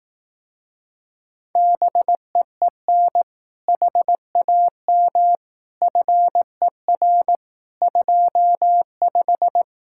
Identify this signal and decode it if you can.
Morse code, heard as a keyed tone, about 18 wpm, 700 Hz. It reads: BEEN HAM FER 25
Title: Speed-Racing Common QSO Phrases Encore - 15wpm